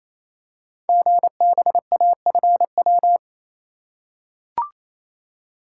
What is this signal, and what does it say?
Morse code, heard as a keyed tone, about 27 wpm, 700 Hz. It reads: GBAFW
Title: Code Groups - 5 Letters - 28wpm